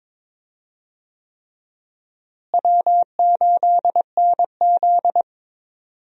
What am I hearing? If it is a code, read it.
Morse code, heard as a keyed tone, about 21 wpm, 700 Hz. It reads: W8NZ
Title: USA Call Signs - 4 Characters - Repeated 3x - 22wpm